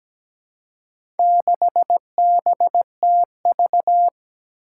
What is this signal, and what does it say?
Morse code, heard as a keyed tone, about 17 wpm, 700 Hz. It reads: 6BTV
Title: Four Challenging Letter-Number Combinations 17wpm